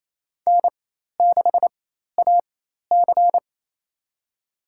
Morse code, heard as a keyed tone, about 28 wpm, spaced as for 13 wpm, 700 Hz. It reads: N6AC
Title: USA Call Signs - 4 Characters - 4x Character Spacing - 28wpm